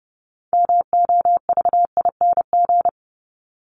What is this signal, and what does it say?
Morse code, heard as a keyed tone, about 30 wpm, 700 Hz. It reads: MOVING
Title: ING Suffix - Top 100 Words with Base Words 30wpm